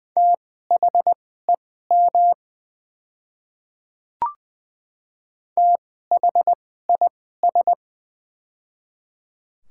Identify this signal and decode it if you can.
Morse code, heard as a keyed tone, about 20 wpm, spaced as for 15 wpm, 700 Hz. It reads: THEM THIS
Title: 2x Character Spacing - Top 100 Words - 20wpm